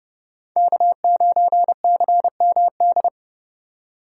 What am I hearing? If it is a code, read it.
Morse code, heard as a keyed tone, about 31 wpm, 700 Hz. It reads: K9CMD